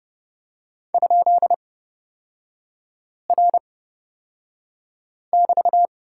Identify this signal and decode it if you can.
Morse code, heard as a keyed tone, about 30 wpm, spaced as for 5 wpm, 700 Hz. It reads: ?R<BT>
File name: Lesson 37e - Letter x - Three Characters 30wpm